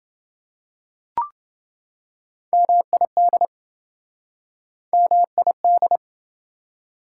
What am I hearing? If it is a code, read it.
Morse code, heard as a keyed tone, about 27 wpm, 700 Hz. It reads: MID MID